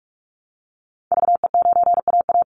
Morse code, heard as a keyed tone, about 44 wpm, 700 Hz. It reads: VE9AA
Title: Call Signs Repeated 3x - 45wpm